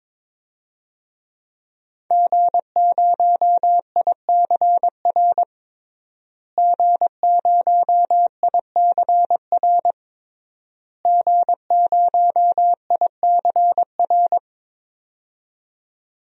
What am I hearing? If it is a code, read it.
Morse code, heard as a keyed tone, about 21 wpm, 700 Hz. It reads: G0ICR G0ICR G0ICR
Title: DXCC-100 Call Signs - 5 Characters - Repeated 3x - 22wpm